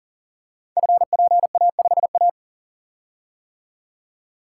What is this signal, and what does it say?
Morse code, heard as a keyed tone, about 38 wpm, 690 Hz. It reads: FPA5A
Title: Code Groups - 5 Letter:Numbers - 40wpm